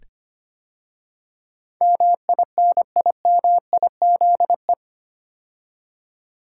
Morse code, heard as a keyed tone, about 25 wpm, 700 Hz. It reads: MINIMIZE
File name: 8 Letter Words - First 3 Letters Given - 25wpm